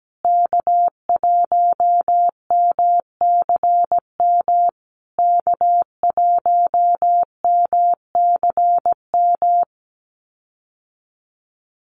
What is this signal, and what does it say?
Morse code, heard as a keyed tone, about 17 wpm, 700 Hz. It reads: K1MCM K1MCM
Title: USA Call Signs Repeated 3x 17wpm